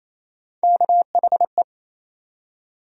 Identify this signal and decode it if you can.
Morse code, heard as a keyed tone, about 27 wpm, 700 Hz. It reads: KHE
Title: USA Call Sign Postfixes - 3 Letters 28wpm